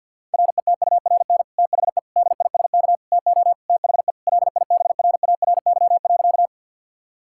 Morse code, heard as a keyed tone, about 50 wpm, 700 Hz. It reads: RETURN THE DISK TO THE LIBRARY.
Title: Sentences from Top 1000 Words - Review Part 6 - 50wpm